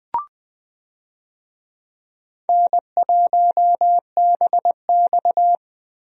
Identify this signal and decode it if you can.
Morse code, heard as a keyed tone, about 19 wpm, 700 Hz. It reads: N1BX
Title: USA Call Signs - 4 Characters - Repeated 3x - 20wpm